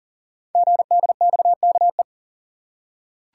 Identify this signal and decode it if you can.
Morse code, heard as a keyed tone, about 38 wpm, 690 Hz. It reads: GDXKE
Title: Code Groups - 5 Letters - 40wpm